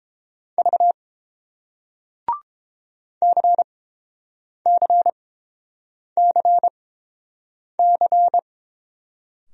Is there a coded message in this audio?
Yes: Morse code from 0.6 to 8.4 s, heard as a keyed tone, about 30 wpm, spaced as for 7 wpm, 700 Hz. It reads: V CCCC